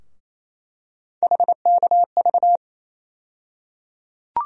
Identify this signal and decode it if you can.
Morse code, heard as a keyed tone, about 28 wpm, 700 Hz. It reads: HKV
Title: USA Call Sign Postfixes - 3 Letters 28wpm